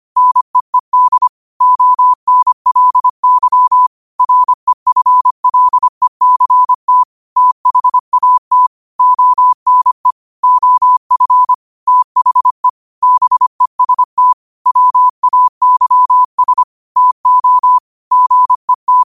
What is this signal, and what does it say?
Morse code, heard as a keyed tone, about 25 wpm, 1000 Hz. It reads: NEED ONLY REFLECT THAT ONE OF THE BEST WAYS TO GET
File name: qotdfiles / 25wpm